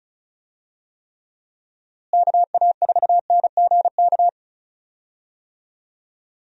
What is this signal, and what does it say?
Morse code, heard as a keyed tone, about 34 wpm, 700 Hz. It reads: KA4NGK
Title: USA Call Signs - 6 Characters - Repeated 3x - 35wpm